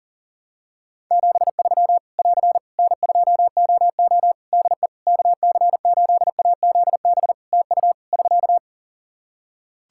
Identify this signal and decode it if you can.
Morse code, heard as a keyed tone, about 40 wpm, 700 Hz. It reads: 73 <AR> N2OO DE KC8AZB TU <SK>
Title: Practice QSO - Exchange 4 - Ending - 40wpm